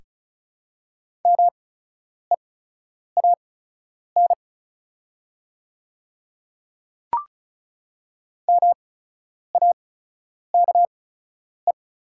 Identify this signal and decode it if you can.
Morse code, heard as a keyed tone, about 36 wpm, spaced as for 10 wpm, 700 Hz. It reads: MEAN MAKE
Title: 8x Character Spacing - Top 200 Words - 35wpm